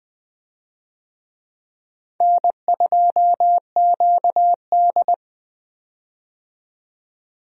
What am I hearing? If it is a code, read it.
Morse code, heard as a keyed tone, about 20 wpm, 700 Hz. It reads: N2QD
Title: USA Call Signs - 4 Characters - 20wpm